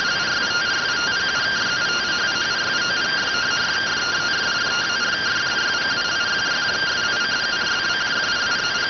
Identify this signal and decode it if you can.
Two-tone teleprinter signal, about 170 Hz shift.